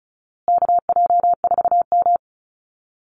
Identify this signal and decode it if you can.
Morse code, heard as a keyed tone, about 34 wpm, 700 Hz. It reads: KJ4M
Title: Call Signs Repeated 2x - 35wpm